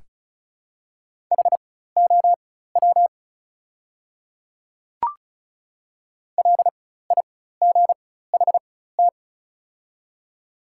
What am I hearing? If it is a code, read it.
Morse code, heard as a keyed tone, about 36 wpm, spaced as for 16 wpm, 700 Hz. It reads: HOW LIGHT
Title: 4x Character Spacing - Top 300 Words - 35wpm